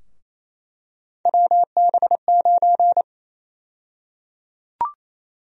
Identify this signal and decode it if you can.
Morse code, heard as a keyed tone, about 28 wpm, 700 Hz. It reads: WB9
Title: USA Call Sign Prefixes - Repeated 2x - 28wpm